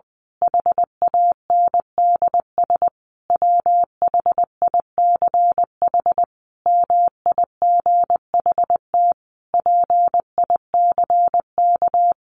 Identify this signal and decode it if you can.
Morse code, heard as a keyed tone, about 20 wpm, 700 Hz. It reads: HANDS WHICH MIGHT PICK
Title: A Sound Of Thunder 20wpm